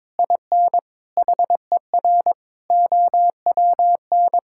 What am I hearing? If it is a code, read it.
Morse code, heard as a keyed tone, about 21 wpm, 700 Hz. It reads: IN HER OWN
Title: Sentences from Top 1600 Words 22wpm